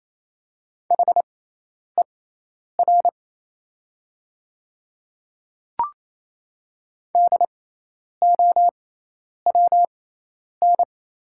Morse code, heard as a keyed tone, about 28 wpm, spaced as for 10 wpm, 700 Hz. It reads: HER DOWN
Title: 6x Character Spacing - Top 100 Words - 28wpm